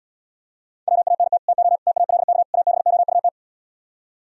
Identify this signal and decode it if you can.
Morse code, heard as a keyed tone, about 77 wpm, 700 Hz. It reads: WANT TO TALK ABOUT